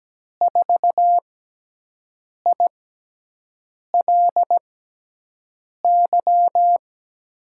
Morse code, heard as a keyed tone, about 17 wpm, 700 Hz. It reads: 4 I L Y
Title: USA Call Signs - 6 Characters - 6x Character Spacing - 17wpm